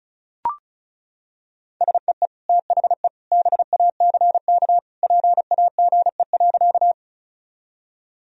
Morse code, heard as a keyed tone, about 36 wpm, 700 Hz. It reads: SEE THE BACK PAGE.
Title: Sentences from Top 1000 Words - Review Part 1 - 35wpm